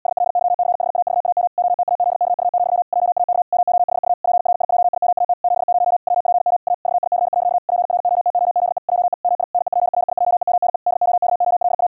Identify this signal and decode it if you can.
Morse code, heard as a keyed tone, about 80 wpm, 710 Hz. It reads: SLYCHAVAM OTAZKU, JAK MOHU PRIJIMAT 70 WPM A VIC. PROTOZE JE MI NEPRIJEMNE DOKOLA
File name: hamradio / clanek-qrq-70wpm / 80wpm